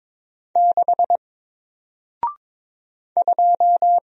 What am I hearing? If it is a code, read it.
Morse code, heard as a keyed tone, about 22 wpm, 700 Hz. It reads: 6 2
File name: Single-Letter-Number-22wpm